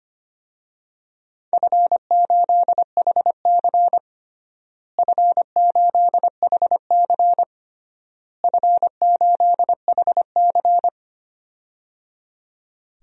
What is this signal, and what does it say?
Morse code, heard as a keyed tone, about 25 wpm, 700 Hz. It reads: F8HC F8HC F8HC
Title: DXCC-100 Call Signs - 4 Characters - Repeated 3x - 25wpm